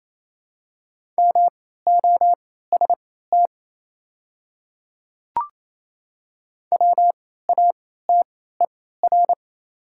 Morse code, heard as a keyed tone, about 27 wpm, spaced as for 16 wpm, 690 Hz. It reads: MOST WATER
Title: 3x Character Spacing - Top 500 Words - 28wpm